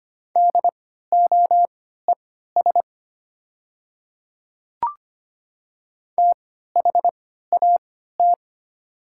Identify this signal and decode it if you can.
Morse code, heard as a keyed tone, about 25 wpm, spaced as for 14 wpm, 690 Hz. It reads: DOES THAT
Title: 3x Character Spacing - Top 200 Words - 25wpm